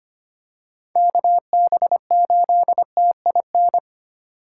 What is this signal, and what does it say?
Morse code, heard as a keyed tone, about 25 wpm, 700 Hz. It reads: KB8TIN
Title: USA Call Signs - 4x Character Spacing - 25wpm